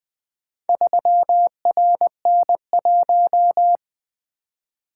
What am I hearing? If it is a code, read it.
Morse code, heard as a keyed tone, about 20 wpm, 700 Hz. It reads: 3RN1
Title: Four Letter-Number Combinations - 20wpm